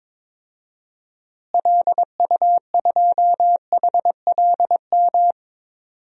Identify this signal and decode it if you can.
Morse code, heard as a keyed tone, about 23 wpm, 700 Hz. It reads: LU2HLM
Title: DXCC-100 Call Signs - 6 Characters - Repeated 3x - 22wpm